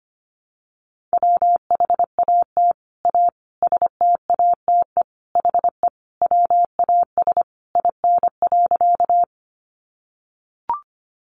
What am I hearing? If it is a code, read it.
Morse code, heard as a keyed tone, about 25 wpm, 700 Hz. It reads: WHAT A STATE HE WAS IN.
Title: Sentences from Top 400 Words 25wpm